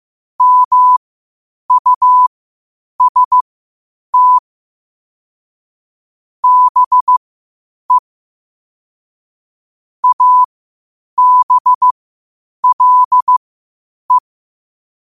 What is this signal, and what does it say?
Morse code, heard as a keyed tone, about 15 wpm, spaced as for 8 wpm, 1000 Hz. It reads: MUST BE ABLE
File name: qotdfiles / 8wpm